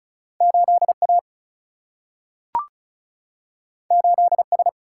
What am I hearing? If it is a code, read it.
Morse code, heard as a keyed tone, about 36 wpm, 700 Hz. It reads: 8A 8S
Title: Single-Letter Single-Number 35wpm